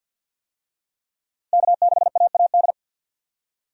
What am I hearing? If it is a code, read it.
Morse code, heard as a keyed tone, about 51 wpm, 700 Hz. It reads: K6AAD
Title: USA Call Signs - 5 Characters - 50wpm